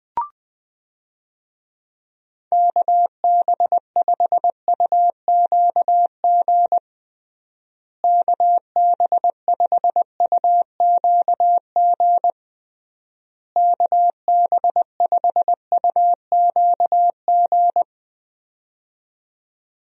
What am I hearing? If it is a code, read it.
Morse code, heard as a keyed tone, about 21 wpm, 700 Hz. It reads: KB5UQG KB5UQG KB5UQG
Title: USA Call Signs - 6 Characters - Repeated 3x - 20wpm